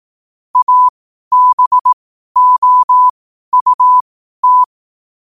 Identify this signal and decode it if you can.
Morse code, heard as a keyed tone, about 18 wpm, spaced as for 12 wpm, 1000 Hz. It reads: ABOUT
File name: 12wpm